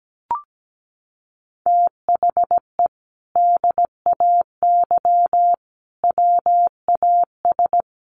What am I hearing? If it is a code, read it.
Morse code, heard as a keyed tone, about 17 wpm, 700 Hz. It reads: THE DAY WAS
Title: Sentences from Top 1200 Words 17wpm